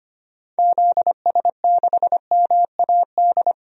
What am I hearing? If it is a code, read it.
Morse code, heard as a keyed tone, about 25 wpm, 700 Hz. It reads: ZS6MAD
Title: DXCC-100 Call Signs - 6 Characters - Repeated 2x - 25wpm